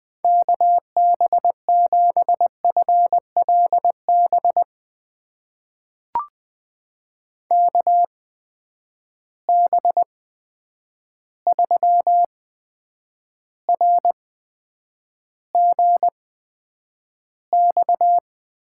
Morse code, heard as a keyed tone, about 20 wpm, spaced as for 9 wpm, 700 Hz. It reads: KB7FLB KB3RGX
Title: USA Call Signs - 6 Characters - 8x Character Spacing - 20wpm